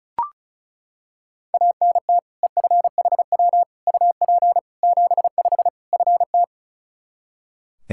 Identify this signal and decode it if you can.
Morse code, heard as a keyed tone, about 36 wpm, 700 Hz. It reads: ANT EFHW UP 75 FT
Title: Practice QSO - Exchange 2 - Info - 35wpm